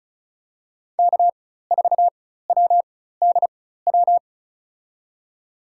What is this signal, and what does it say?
Morse code, heard as a keyed tone, about 36 wpm, spaced as for 16 wpm, 700 Hz. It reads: K4WDW
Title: USA Call Signs - 4x Character Spacing - 35wpm